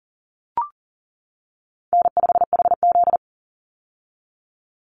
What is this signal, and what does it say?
Morse code, heard as a keyed tone, about 40 wpm, 700 Hz. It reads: N5HZ
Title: Call Signs - 40wpm